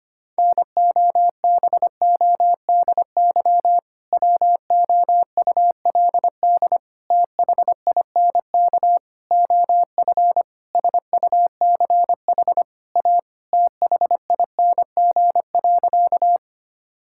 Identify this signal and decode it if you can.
Morse code, heard as a keyed tone, about 25 wpm, 700 Hz. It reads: NOBODY WOULD THINK OF SUCH A THING.
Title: Sentences from Top 1600 Words 25wpm